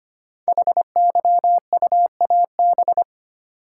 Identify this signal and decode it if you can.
Morse code, heard as a keyed tone, about 25 wpm, 700 Hz. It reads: HYUAB
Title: Code Groups - 5 Letter:Numbers - 25wpm